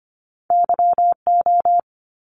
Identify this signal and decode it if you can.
Morse code, heard as a keyed tone, about 25 wpm, 700 Hz. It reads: YO
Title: Top Two Letter Combinations in Words 25wpm